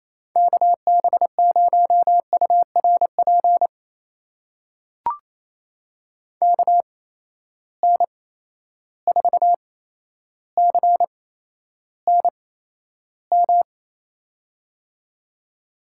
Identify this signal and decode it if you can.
Morse code, heard as a keyed tone, about 28 wpm, spaced as for 12 wpm, 700 Hz. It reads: KB0URP KN4CNM
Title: USA Call Signs - 6 Characters - 8x Character Spacing - 28wpm